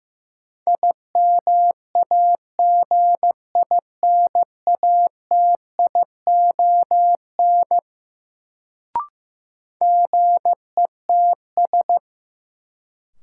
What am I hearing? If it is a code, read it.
Morse code, heard as a keyed tone, about 15 wpm, 700 Hz. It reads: IMAGINATION GETS
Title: Sentences from Top 1300 Words - New Word Drill - 15wpm